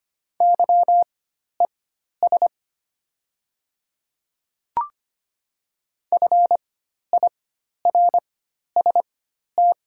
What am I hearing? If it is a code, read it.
Morse code, heard as a keyed tone, about 25 wpm, spaced as for 12 wpm, 700 Hz. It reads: YES FIRST
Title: 4x Character Spacing - Top 500 Words - 25wpm